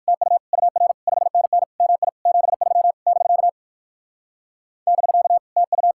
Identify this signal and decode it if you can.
Morse code, heard as a keyed tone, about 58 wpm, 700 Hz. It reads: TU UR 5NN MI 73 <BK> <BK> TU